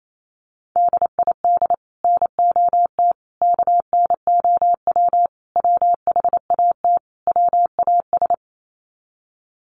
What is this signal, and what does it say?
Morse code, heard as a keyed tone, about 28 wpm, 700 Hz. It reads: DID NOT KNOW WHAT WAS
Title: Sets of 5 Words Encore2 28wpm